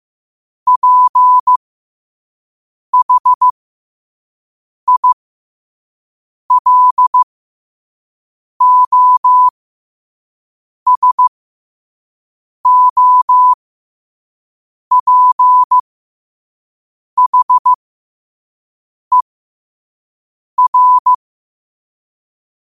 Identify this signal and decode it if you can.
Morse code, heard as a keyed tone, about 15 wpm, spaced as for 5 wpm, 1000 Hz. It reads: PHILOSOPHER
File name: qotdfiles / 5wpm